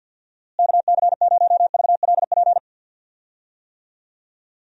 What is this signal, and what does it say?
Morse code, heard as a keyed tone, about 52 wpm, 700 Hz. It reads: KC0VLP
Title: USA Call Signs - 6 Characters - Repeated 2x - 50wpm